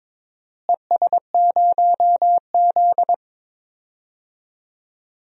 Morse code, heard as a keyed tone, about 22 wpm, 700 Hz. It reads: ES0Z
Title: DXCC-100 Call Signs - 4 Characters - Repeated 2x - 22wpm